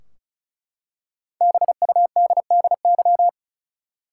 Morse code, heard as a keyed tone, about 34 wpm, 700 Hz. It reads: BUDDY
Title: Five Letter Words - 35wpm